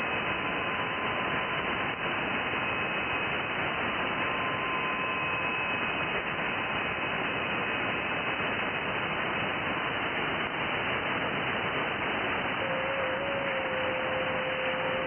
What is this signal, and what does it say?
FT8.
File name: IC-7100